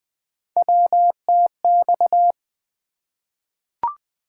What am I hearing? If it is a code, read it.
Morse code, heard as a keyed tone, about 20 wpm, 700 Hz. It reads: WTX
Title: ARRL Sections v2 - 20wpm